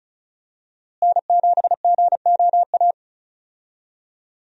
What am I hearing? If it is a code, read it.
Morse code, heard as a keyed tone, about 36 wpm, 700 Hz. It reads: N7GOA
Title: USA Call Signs - 5 Characters - Repeated 2x - 35wpm